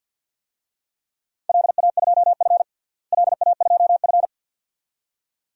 Morse code, heard as a keyed tone, about 52 wpm, 700 Hz. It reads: LA2F LA2F
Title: DXCC-100 Call Signs - 4 Characters - Repeated 2x - 50wpm